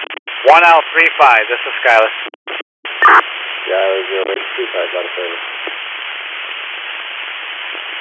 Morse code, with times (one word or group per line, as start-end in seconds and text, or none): none